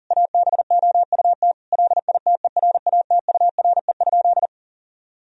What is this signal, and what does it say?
Morse code, heard as a keyed tone, about 40 wpm, 700 Hz. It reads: ABOUT LITERATURE?
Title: Sentences from Top 1000 Words - Review Part 3 - 40wpm